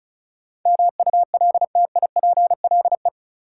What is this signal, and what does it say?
Morse code, heard as a keyed tone, about 36 wpm, 700 Hz. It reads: MULTIPLE
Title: Sets of 5 Words - New Word Drill - 35wpm